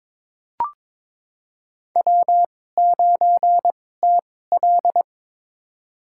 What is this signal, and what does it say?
Morse code, heard as a keyed tone, about 22 wpm, spaced as for 16 wpm, 700 Hz. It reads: W9TL
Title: USA Call Signs - 4 Characters - 2x Character Spacing - 22wpm